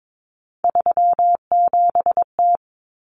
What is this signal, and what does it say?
Morse code, heard as a keyed tone, about 22 wpm, 700 Hz. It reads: 37T
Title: Common Cut Numbers - 3-Digit 22wpm